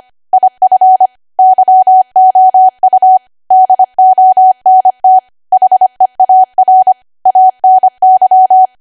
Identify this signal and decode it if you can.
Morse code, heard as a keyed tone, about 25 wpm, 750 Hz. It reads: IF YOU DONT HEAR ANY
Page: 25 WPM Code Archive